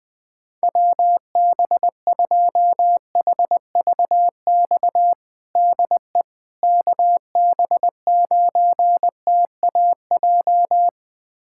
Morse code, heard as a keyed tone, about 21 wpm, 700 Hz. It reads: WB2HVX DE KB9TAJ
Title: USA Callsign DE Callsign - 6 Characters - 20wpm